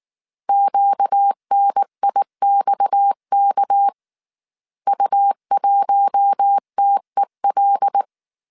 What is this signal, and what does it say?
Morse code, heard as a keyed tone, about 19 wpm, 800 Hz. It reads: QNIXK U1TEL
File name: CW / DET / 1 6 1500C S A 18wpm 2255